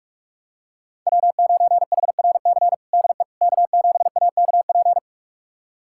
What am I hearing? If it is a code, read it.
Morse code, heard as a keyed tone, about 46 wpm, 700 Hz. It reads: W9HRC DE K7AKP
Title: USA Callsign DE Callsign - 5 Characters - 45wpm